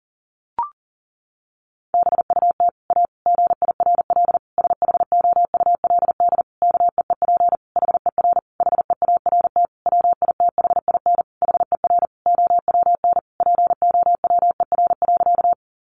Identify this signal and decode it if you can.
Morse code, heard as a keyed tone, about 40 wpm, 700 Hz. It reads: BUT A GIRL SHOULD KEEP HER HEART WITHIN HER OWN POWER.
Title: Sentences from Top 1200 Words 40wpm